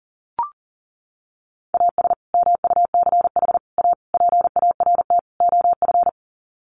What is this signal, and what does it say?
Morse code, heard as a keyed tone, about 40 wpm, 700 Hz. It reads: AS MUCH A PART OF
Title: Sets of 5 Words Encore2 40wpm